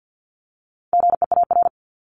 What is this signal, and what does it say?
Morse code, heard as a keyed tone, about 50 wpm, 700 Hz. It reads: GEAR
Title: Top 1251-1500 Words 50wpm